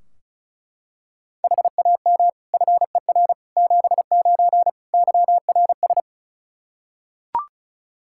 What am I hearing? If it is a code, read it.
Morse code, heard as a keyed tone, about 36 wpm, 700 Hz. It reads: HAM FER 79 YRS
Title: Practice QSO - Exchange 3 - More Info - 35wpm